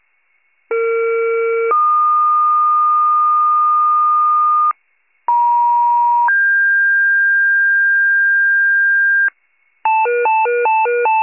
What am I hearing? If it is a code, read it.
Two-tone page, heard as a one-second tone, then a three-second tone, about 460 then 1220 Hz, about 950 then 1600 Hz, with noise. It is heard from 0.7 to 9.3 s.
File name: wcalert